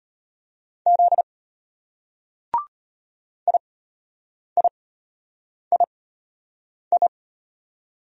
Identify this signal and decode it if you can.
Morse code, heard as a keyed tone, about 34 wpm, spaced as for 8 wpm, 700 Hz. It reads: Z IIII